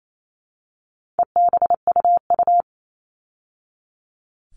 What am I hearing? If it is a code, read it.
Morse code, heard as a keyed tone, about 28 wpm, 700 Hz. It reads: EBUU
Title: All Cut Numbers - 4-Digit 28wpm